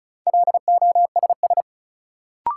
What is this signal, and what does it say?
Morse code, heard as a keyed tone, about 36 wpm, 700 Hz. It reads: LOSS
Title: Sentences from Top 1400 Words - New Word Drill - 35wpm